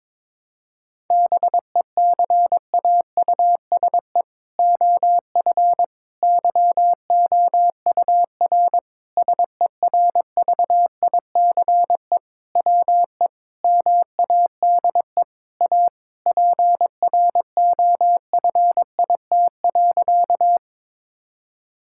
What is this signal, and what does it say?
Morse code, heard as a keyed tone, about 22 wpm, 700 Hz. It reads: BECAUSE OF YOUR SERVICE WE MADE A PROFIT.
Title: Sentences from Top 1000 Words - Review Part 5 - 22wpm